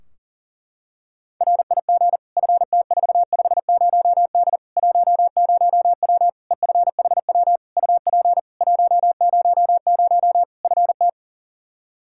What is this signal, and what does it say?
Morse code, heard as a keyed tone, about 38 wpm, 700 Hz. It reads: RIG FT450D 10W EFHW UP 100 FT
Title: K1USN SST QSO Practice 40wpm